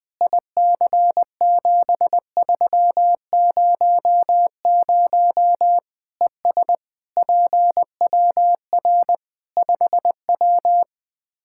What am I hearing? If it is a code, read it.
Morse code, heard as a keyed tone, about 20 wpm, 700 Hz. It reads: IC7300 ES PWR 5W
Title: Practice QSO - Exchange 2 - Info - 20wpm